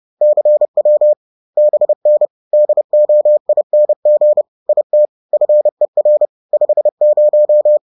Morse code, heard as a keyed tone, about 30 wpm, 600 Hz. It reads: CW BN DOING IT FER 50